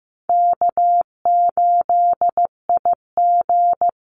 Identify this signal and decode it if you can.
Morse code, heard as a keyed tone, about 15 wpm, 700 Hz. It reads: K8IG
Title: USA Call Signs Repeated 3x 15wpm